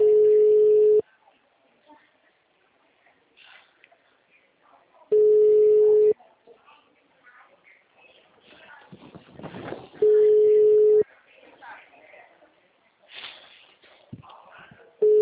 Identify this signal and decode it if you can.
Ringback tone (Europe, 425 Hz), with speech.